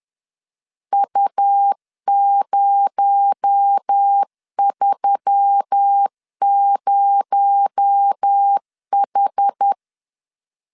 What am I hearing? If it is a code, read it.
Morse code, heard as a keyed tone, about 10 wpm, 800 Hz. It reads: U030H
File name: LEZIONI / 6 PKX0 / 6 PKX0 10wpm 2707